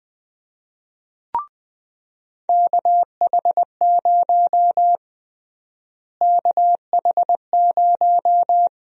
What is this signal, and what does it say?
Morse code, heard as a keyed tone, about 20 wpm, 700 Hz. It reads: KH0 KH0
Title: USA Call Sign Prefixes - Repeated 2x - 20wpm